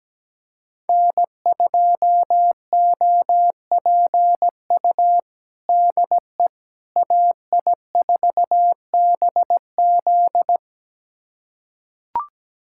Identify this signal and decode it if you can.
Morse code, heard as a keyed tone, about 17 wpm, 700 Hz. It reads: N2OPU DE AI4BZ
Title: USA Callsign DE Callsign 17wpm